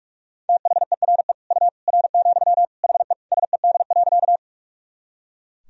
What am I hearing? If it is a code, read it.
Morse code, heard as a keyed tone, about 45 wpm, 700 Hz. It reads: THERE U R, HE SED.
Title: Sentences from Top 100 Words and Abbreviations 45wpm